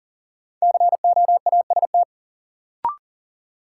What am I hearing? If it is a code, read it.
Morse code, heard as a keyed tone, about 40 wpm, 700 Hz. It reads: COAST
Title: Sentences from Top 1300 Words - New Word Drill Alt - 40wpm